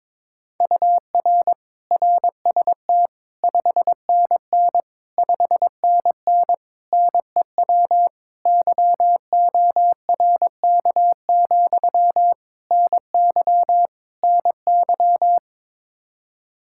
Morse code, heard as a keyed tone, about 22 wpm, 700 Hz. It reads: UR RST 5NN 5NN NEW YORK, NY NY
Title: Practice QSO - Exchange 1 - Beginning - 22wpm